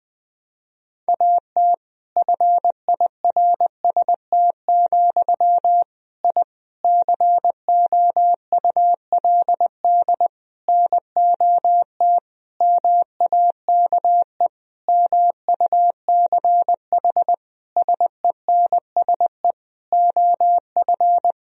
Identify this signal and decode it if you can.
Morse code, heard as a keyed tone, about 20 wpm, 700 Hz. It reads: AT FIRST, I COULD NOT MAKE MUCH SENSE OF
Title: Sentences from Top 1000 Words - Review Part 4 - 20wpm